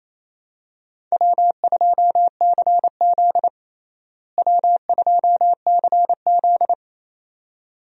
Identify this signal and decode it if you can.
Morse code, heard as a keyed tone, about 28 wpm, 700 Hz. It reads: W2CZ W2CZ
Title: USA Call Signs - 4 Characters - Repeated 2x - 28wpm